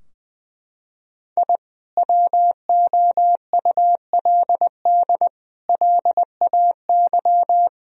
Morse code, heard as a keyed tone, about 20 wpm, 700 Hz. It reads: I WOULD LAY
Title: Sentences from Top 1000 Words - Review Part 2 - 20wpm